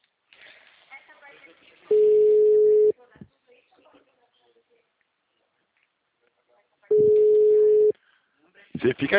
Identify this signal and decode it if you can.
Ringback tone (Europe, 425 Hz), with speech.
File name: out- -2003- - - .38983